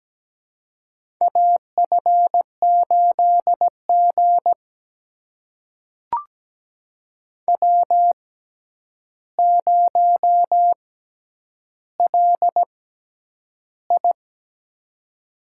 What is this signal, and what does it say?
Morse code, heard as a keyed tone, about 17 wpm, spaced as for 9 wpm, 700 Hz. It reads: AF8G W0LI
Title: USA Call Signs - 4 Characters - 6x Character Spacing - 17wpm